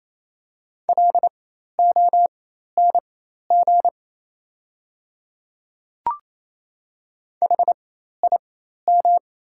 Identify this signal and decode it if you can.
Morse code, heard as a keyed tone, about 27 wpm, spaced as for 13 wpm, 690 Hz. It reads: LONG HIM
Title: 4x Character Spacing - Top 100 Words - 28wpm